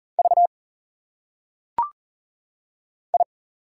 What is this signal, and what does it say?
Morse code, heard as a keyed tone, about 40 wpm, 700 Hz. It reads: V I
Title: Single Letters - Phonetic Pronunciation 40wpm